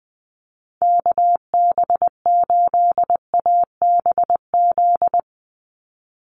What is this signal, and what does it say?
Morse code, heard as a keyed tone, about 20 wpm, 700 Hz. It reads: KB8ABZ
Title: USA Call Signs 20wpm